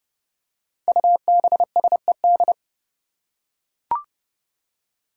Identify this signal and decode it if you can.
Morse code, heard as a keyed tone, about 30 wpm, 700 Hz. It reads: UBSED
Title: Five Challenging Letter-Number Combinations 30wpm